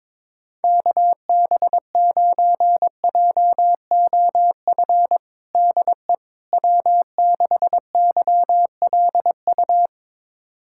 Morse code, heard as a keyed tone, about 22 wpm, 700 Hz. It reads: KB9JOF DE W6YLU
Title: USA Callsign DE Callsign 22wpm